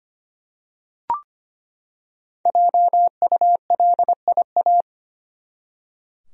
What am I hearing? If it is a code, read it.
Morse code, heard as a keyed tone, about 25 wpm, 700 Hz. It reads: JULIA
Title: USA Common Names - Max 5 Letters - 25wpm